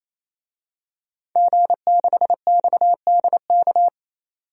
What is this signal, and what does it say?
Morse code, heard as a keyed tone, about 27 wpm, 700 Hz. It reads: G6XDK
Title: DXCC-100 Call Signs - 5 Characters - 28wpm